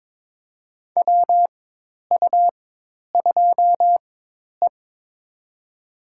Morse code, heard as a keyed tone, about 22 wpm, spaced as for 10 wpm, 700 Hz. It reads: WU2E
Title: USA Call Signs - 4x Character Spacing - 22wpm